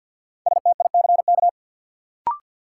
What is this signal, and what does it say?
Morse code, heard as a keyed tone, about 53 wpm, 700 Hz. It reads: STICK
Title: Sentences from Top 1200 Words - New Word Drill - 50wpm